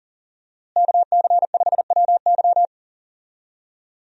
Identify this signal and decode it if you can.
Morse code, heard as a keyed tone, about 37 wpm, 700 Hz. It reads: KC5WY
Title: USA Call Signs - 5 Characters - Repeated 3x - 40wpm